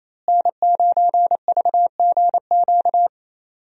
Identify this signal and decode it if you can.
Morse code, heard as a keyed tone, about 28 wpm, 700 Hz. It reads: N9VGQ
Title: USA Call Signs - 5 Characters - Repeated 2x - 28wpm